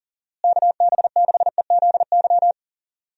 Morse code, heard as a keyed tone, about 40 wpm, 700 Hz. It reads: KB6EZY
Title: USA Call Signs - 6 Characters - Repeated 3x - 40wpm